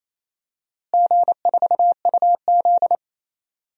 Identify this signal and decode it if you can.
Morse code, heard as a keyed tone, about 28 wpm, 700 Hz. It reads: G4UZ
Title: DXCC-100 Call Signs - 4 Characters - Repeated 2x - 28wpm